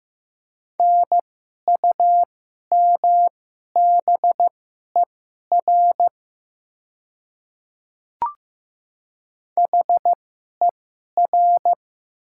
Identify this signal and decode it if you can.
Morse code, heard as a keyed tone, about 15 wpm, spaced as for 11 wpm, 700 Hz. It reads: NUMBER HER
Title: 2x Character Spacing - Top 100 Words - 15wpm